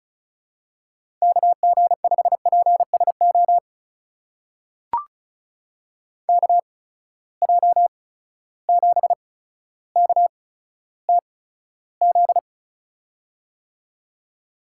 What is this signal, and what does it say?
Morse code, heard as a keyed tone, about 36 wpm, spaced as for 15 wpm, 700 Hz. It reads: KG5PSO KJ7KTZ
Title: USA Call Signs - 8x Character Spacing - 35wpm